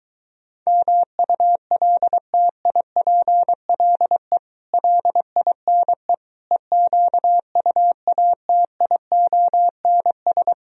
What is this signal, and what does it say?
Morse code, heard as a keyed tone, about 23 wpm, 700 Hz. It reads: MULTIPLE LINE EQUATIONS